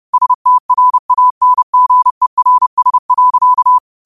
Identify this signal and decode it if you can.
Morse code, heard as a keyed tone, about 30 wpm, 1000 Hz. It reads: STRANGERS.
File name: qotdfiles / 30wpm